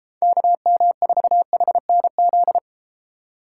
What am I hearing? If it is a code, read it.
Morse code, heard as a keyed tone, about 32 wpm, 700 Hz. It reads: KM4HNZ